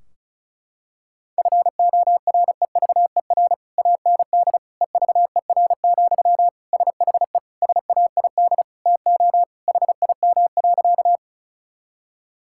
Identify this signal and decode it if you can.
Morse code, heard as a keyed tone, about 34 wpm, 700 Hz. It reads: FOREVER AND EVER, SHE SAID TO HIM.
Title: Sentences from Top 1600 Words 35wpm